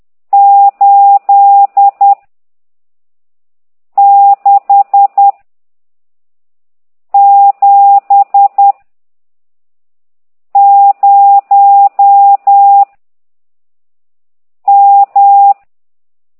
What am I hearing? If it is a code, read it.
Morse code, heard as a keyed tone, about 10 wpm, 800 Hz. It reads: 8 6 7 0 M
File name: ct2idw / exerc15